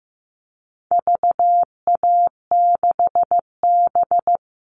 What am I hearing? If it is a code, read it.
Morse code, heard as a keyed tone, about 15 wpm, 700 Hz. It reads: VA6B
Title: All Cut Numbers - 4-Digit 15wpm